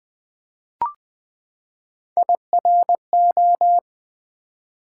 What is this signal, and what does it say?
Morse code, heard as a keyed tone, about 20 wpm, 700 Hz. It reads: IRO
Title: Build A Word - 3 Letters At A Time - 6-Letter Words - 20wpm